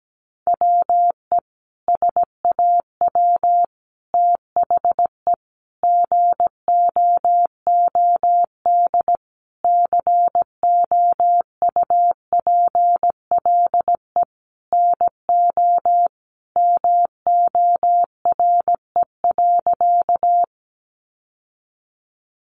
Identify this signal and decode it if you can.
Morse code, heard as a keyed tone, about 17 wpm, 700 Hz. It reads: WE SAW THE GOOD COUPLE NO MORE.
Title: Sentences from Top 1100 Words 17wpm